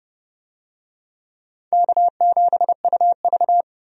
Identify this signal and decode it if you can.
Morse code, heard as a keyed tone, about 31 wpm, 700 Hz. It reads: K7UV
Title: USA Call Signs - 4 Characters - 30wpm